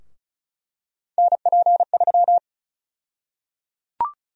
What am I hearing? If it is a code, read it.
Morse code, heard as a keyed tone, about 34 wpm, 700 Hz. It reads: NP3
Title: USA Call Sign Prefixes - Repeated 2x - 35wpm